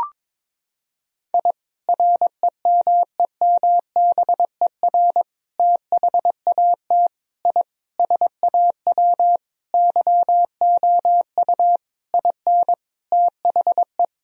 Morse code, heard as a keyed tone, about 22 wpm, 700 Hz. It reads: I REMEMBER THAT I SAW YOU IN THE
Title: Sentences from Top 1000 Words - Review Part 1 - 22wpm